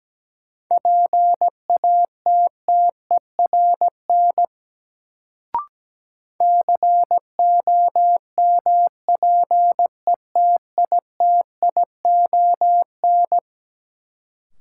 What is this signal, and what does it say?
Morse code, heard as a keyed tone, about 17 wpm, 700 Hz. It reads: PATTERN COMPETITION
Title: Sentences from Top 1000 Words - New Word Drill - 17wpm